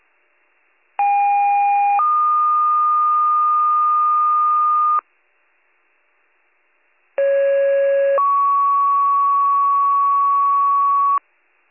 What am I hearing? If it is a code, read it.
Two-tone page, heard as a one-second tone, then a three-second tone, about 810 then 1220 Hz, about 550 then 1120 Hz. It is heard from 1.0 to 11.2 s.